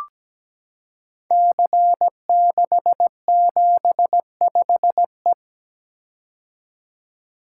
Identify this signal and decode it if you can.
Morse code, heard as a keyed tone, about 17 wpm, 700 Hz. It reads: C675E
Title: Code Groups - 5 Letter:Numbers - 17wpm